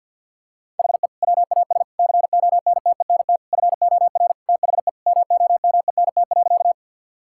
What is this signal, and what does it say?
Morse code, heard as a keyed tone, about 52 wpm, 700 Hz. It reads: HE WAS CONTENT FOR THE MOMENT.
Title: Sentences from Top 1000 Words - Review Part 6 - 50wpm